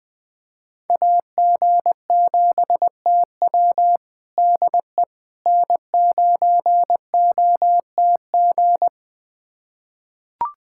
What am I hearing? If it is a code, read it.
Morse code, heard as a keyed tone, about 20 wpm, 700 Hz. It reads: AG7TW DE N9OTG
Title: USA Callsign DE Callsign - 5 Characters - 20wpm